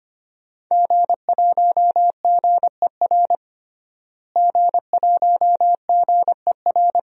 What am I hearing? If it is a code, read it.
Morse code, heard as a keyed tone, about 25 wpm, 700 Hz. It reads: G1GER G1GER
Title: DXCC-100 Call Signs - 5 Characters - Repeated 3x - 25wpm